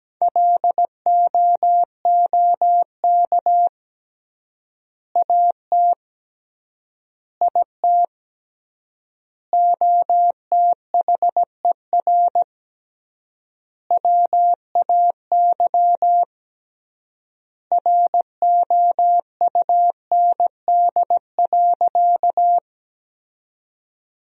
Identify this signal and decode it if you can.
Morse code, heard as a keyed tone, about 17 wpm, 700 Hz. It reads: LOOK AT IT OTHER WAY ROUND.
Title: 3x Word-Spacing - Sentences from Top 200 Words - 17wpm